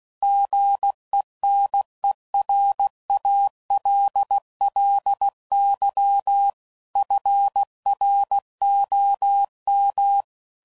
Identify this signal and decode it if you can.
Morse code, heard as a keyed tone, about 16 wpm, 800 Hz. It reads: GENERALLY FROM
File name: CW Code Groups @15WPM / Weather Sets / Weather 4 17